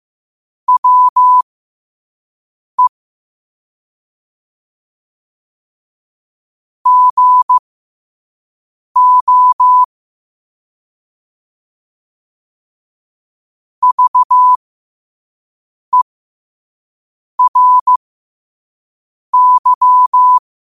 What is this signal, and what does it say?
Morse code, heard as a keyed tone, about 15 wpm, spaced as for 5 wpm, 1000 Hz. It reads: WE GO VERY
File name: 5wpm